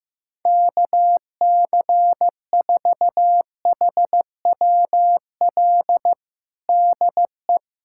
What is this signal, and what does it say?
Morse code, heard as a keyed tone, about 15 wpm, 700 Hz. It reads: KC4HWL DE
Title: USA Callsign DE Callsign 15wpm